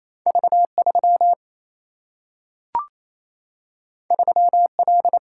Morse code, heard as a keyed tone, about 28 wpm, 700 Hz. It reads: V3 3L
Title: Single-Letter Single-Number 28wpm